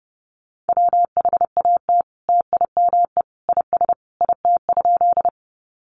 Morse code, heard as a keyed tone, about 29 wpm, 700 Hz. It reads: WHAT TIME IS IT?
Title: Sentences from Top 100 Words 30wpm